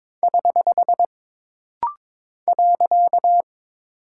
Morse code, heard as a keyed tone, about 22 wpm, 700 Hz. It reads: <HH> .
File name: practice / 22wpm / Single Punctuation-Prosign - Rapid-Fire - 22wpm